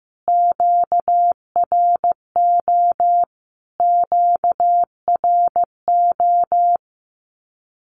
Morse code, heard as a keyed tone, about 15 wpm, 700 Hz. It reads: QRO QRO
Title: Q-Codes Repeated 3x 15wpm